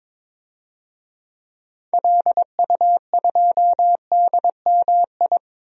Morse code, heard as a keyed tone, about 22 wpm, 700 Hz. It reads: LU2DMI
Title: DXCC-100 Call Signs - 6 Characters - Repeated 3x - 22wpm